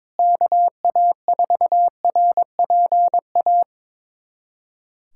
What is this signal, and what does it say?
Morse code, heard as a keyed tone, about 22 wpm, 700 Hz. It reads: KA4RPA
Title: USA Callsign DE Callsign 22wpm